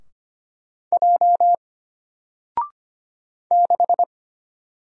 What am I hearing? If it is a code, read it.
Morse code, heard as a keyed tone, about 25 wpm, 700 Hz. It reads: J 6
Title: Single Character (Letter:Number:Punctuation:Prosign) Alt 25wpm